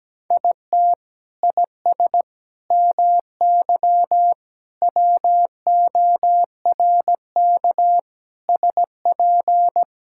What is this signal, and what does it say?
Morse code, heard as a keyed tone, about 17 wpm, 700 Hz. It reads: IT IS MY WORK SP
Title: Sentences from Top 1000 Words - Review Part 5 - 17wpm